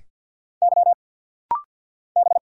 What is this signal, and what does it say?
Morse code, heard as a keyed tone, about 50 wpm, 690 Hz. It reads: Y B
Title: Single Letters - Rapid-Fire 50wpm